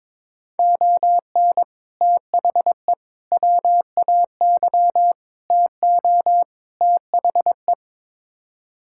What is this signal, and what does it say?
Morse code, heard as a keyed tone, about 21 wpm, 700 Hz. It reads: ON THE WAY TO THE
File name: practice / 22wpm / Build A Phrase - 5-Word Phrases - 22wpm